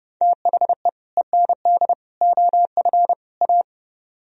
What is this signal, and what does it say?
Morse code, heard as a keyed tone, about 28 wpm, 700 Hz. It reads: THE END OF A